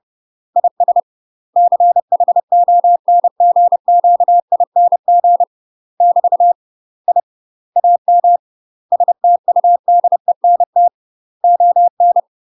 Morse code, heard as a keyed tone, about 30 wpm, 700 Hz. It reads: IS CHONGQING <BT> I AM STUDENT ON
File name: UK MASTER QSOs TRAINING / 30WPM / UK MASTER QSOs 0017